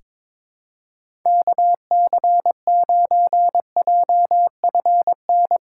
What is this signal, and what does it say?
Morse code, heard as a keyed tone, about 22 wpm, 700 Hz. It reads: KC9JFN
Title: USA Call Signs - 6 Characters - 3x Character Spacing - 22wpm